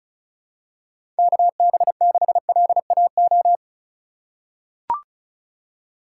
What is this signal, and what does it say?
Morse code, heard as a keyed tone, about 36 wpm, 700 Hz. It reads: KB6LAO
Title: USA Call Signs - 6 Characters - 3x Character Spacing - 35wpm